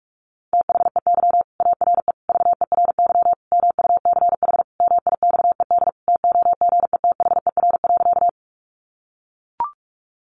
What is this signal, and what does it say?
Morse code, heard as a keyed tone, about 44 wpm, 700 Hz. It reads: THEY ARE VERY MUCH MIXED TOGETHER.
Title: Sentences from Top 1500 Words 45wpm